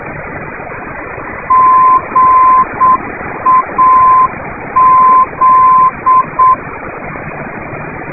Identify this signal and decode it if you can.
Morse code, heard as a keyed tone, about 7 wpm, 1030 Hz, with radio noise. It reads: GAZ
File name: swli05639 / ndb GAZ